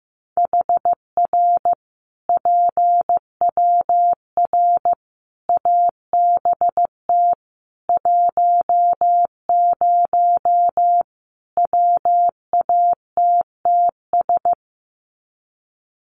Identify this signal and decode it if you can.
Morse code, heard as a keyed tone, about 15 wpm, 700 Hz. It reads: HR PWR ABT 10 WATTS
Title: Common QSO Phrases 15wpm